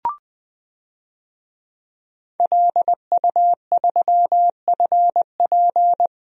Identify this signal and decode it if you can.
Morse code, heard as a keyed tone, about 20 wpm, 700 Hz. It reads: LU3FP
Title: DXCC-100 Call Signs - 5 Characters - Repeated 2x - 20wpm